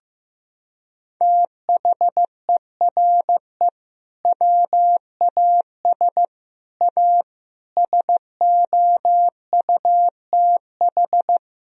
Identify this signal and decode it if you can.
Morse code, heard as a keyed tone, about 15 wpm, 700 Hz. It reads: THERE WAS A SOUTH
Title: Sentences from Top 1000 Words - Review Part 2 - 15wpm